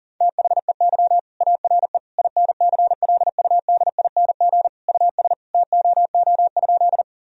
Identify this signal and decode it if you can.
Morse code, heard as a keyed tone, about 40 wpm, 700 Hz. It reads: THEY ARE INCLUDING US TOO?
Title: Sentences from Top 1000 Words - Review Part 4 - 40wpm